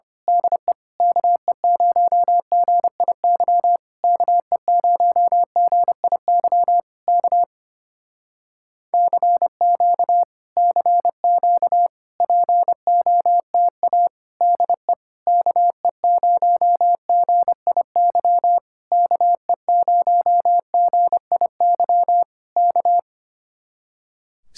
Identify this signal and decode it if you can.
Morse code, heard as a keyed tone, about 25 wpm, 700 Hz. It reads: DE KE0GIY KE0GIY K CQ CQ POTA DE KE0GIY KE0GIY K
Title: Speed-Racing Common QSO Phrases - 25wpm